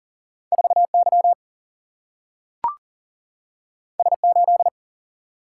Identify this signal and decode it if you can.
Morse code, heard as a keyed tone, about 38 wpm, 700 Hz. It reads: 4 Y S 8
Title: Single-Letter Single-Number 40wpm